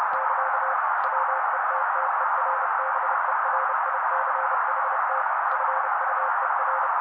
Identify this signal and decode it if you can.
Morse code, heard as a keyed tone, about 30 wpm, 530 Hz. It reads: <KA> WETTERBERICHT FUER
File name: daily / 29wpm noise